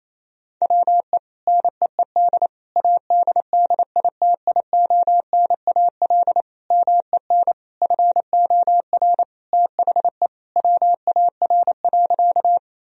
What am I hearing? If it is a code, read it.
Morse code, heard as a keyed tone, about 28 wpm, 700 Hz. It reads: WE NEED ADDITIONAL MEN FOR THE WAR.
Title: Sentences from Top 1000 Words - Review Part 4 - 28wpm